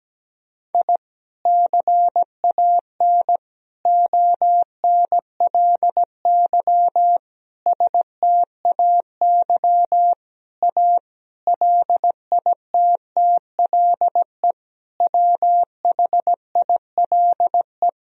Morse code, heard as a keyed tone, about 17 wpm, 700 Hz. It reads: I CAN ONLY STAY A LITTLE WHILE
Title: Sentences from Top 1000 Words - Review Part 3 - 17wpm